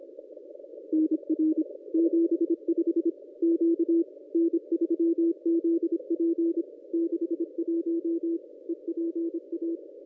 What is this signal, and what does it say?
Morse code, heard as a keyed tone, about 26 wpm, 330 Hz. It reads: NR 75 Q N3ZP 61 EPA
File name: contests / ss-cw-k5tr-op-n5rz -1100